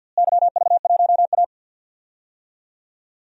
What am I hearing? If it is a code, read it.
Morse code, heard as a keyed tone, about 51 wpm, 700 Hz. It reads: YV1A
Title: DXCC-100 Call Signs - 4 Characters - Repeated 2x - 50wpm